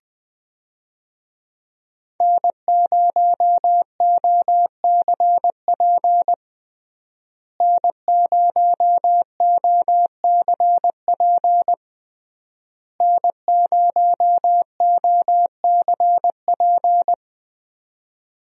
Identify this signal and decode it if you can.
Morse code, heard as a keyed tone, about 21 wpm, 700 Hz. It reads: N0OCP N0OCP N0OCP
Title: USA Call Signs - 5 Characters - Repeated 3x - 20wpm